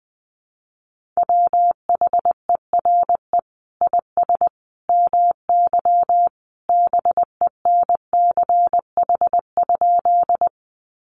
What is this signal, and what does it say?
Morse code, heard as a keyed tone, about 20 wpm, 700 Hz. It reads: WHERE IS MY BENCH?
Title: Sentences from Top 1400 Words 20wpm